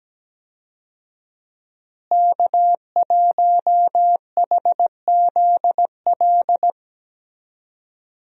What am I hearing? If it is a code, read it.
Morse code, heard as a keyed tone, about 17 wpm, 700 Hz. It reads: K1HZL
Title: USA Call Signs - 5 Characters - Repeated 3x - 17wpm